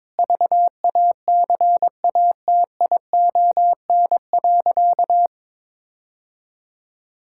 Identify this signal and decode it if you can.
Morse code, heard as a keyed tone, about 21 wpm, 700 Hz. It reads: VACATION.
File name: Difficult Sentences with Hints - 22wpm